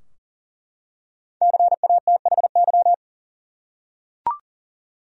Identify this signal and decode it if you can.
Morse code, heard as a keyed tone, about 40 wpm, 700 Hz. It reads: CATHY
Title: USA Common Names - Max 5 Letters - 40wpm